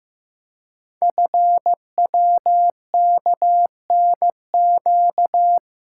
Morse code, heard as a keyed tone, about 15 wpm, 700 Hz. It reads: FWKNQ
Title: Code Groups - 5 Letters - 15wpm